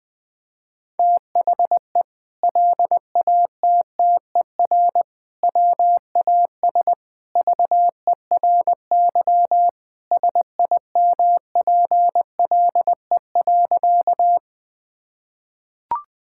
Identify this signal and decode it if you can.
Morse code, heard as a keyed tone, about 21 wpm, 700 Hz. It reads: THE LATTER WAS VERY SIMPLE.
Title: Sentences from Top 1600 Words 20wpm